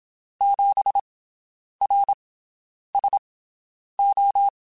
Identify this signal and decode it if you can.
Morse code, heard as a keyed tone, about 27 wpm, spaced as for 9 wpm, 800 Hz. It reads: 7RSO
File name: QSOs Spaced